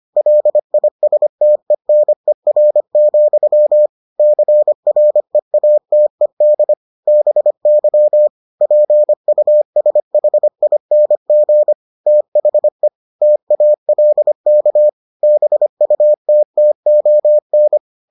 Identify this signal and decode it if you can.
Morse code, heard as a keyed tone, about 25 wpm, 600 Hz. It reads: LISTENER, CREATED BY PUSHING THE TALK BUTTON